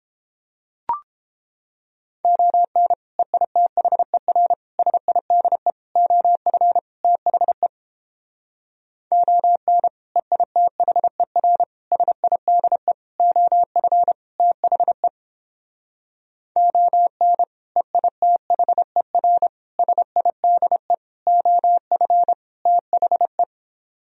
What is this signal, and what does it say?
Morse code, heard as a keyed tone, about 29 wpm, 700 Hz. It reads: ON EITHER SIDE OF THE ON EITHER SIDE OF THE ON EITHER SIDE OF THE